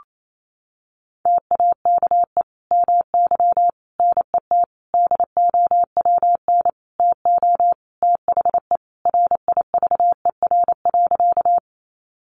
Morse code, heard as a keyed tone, about 28 wpm, 700 Hz. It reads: TAKE MY NET DOWN TO THE RIVER.
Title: Sentences from Top 1400 Words 28wpm